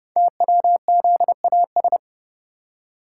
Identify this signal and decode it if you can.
Morse code, heard as a keyed tone, about 28 wpm, 700 Hz. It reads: TWZAS
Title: Code Groups - 5 Letters - 30wpm